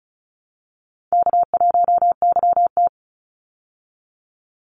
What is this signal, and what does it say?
Morse code, heard as a keyed tone, about 34 wpm, 700 Hz. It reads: K1YT
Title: Call Signs Repeated 3x - 35wpm